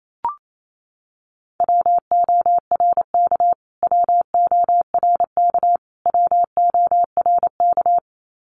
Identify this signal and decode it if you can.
Morse code, heard as a keyed tone, about 28 wpm, 700 Hz. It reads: WORK WORK WORK
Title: Top 200 Words Repeated 28wpm